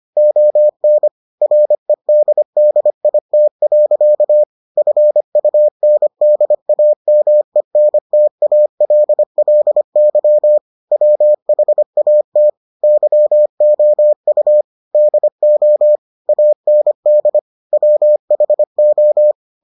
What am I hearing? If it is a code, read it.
Morse code, heard as a keyed tone, about 25 wpm, 600 Hz. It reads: ON REDDIT. FUNDAMENTALLY WHAT YOU DO AND WHO